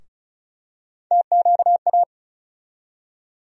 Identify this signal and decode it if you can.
Morse code, heard as a keyed tone, about 34 wpm, 700 Hz. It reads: TQA
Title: USA Call Sign Postfixes - 3 Letters 35wpm